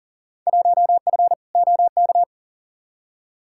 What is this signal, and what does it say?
Morse code, heard as a keyed tone, about 40 wpm, 700 Hz. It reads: 1F OK
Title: Field Day Exchanges Encore - 40wpm